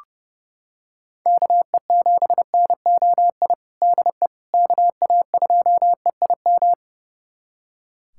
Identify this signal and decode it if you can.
Morse code, heard as a keyed tone, about 30 wpm, 700 Hz. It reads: KE7NOI DE KA2EIM
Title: USA Callsign DE Callsign - 6 Characters - 30wpm